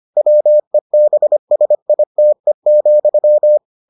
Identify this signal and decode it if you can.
Morse code, heard as a keyed tone, about 25 wpm, 600 Hz. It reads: WEBSITE,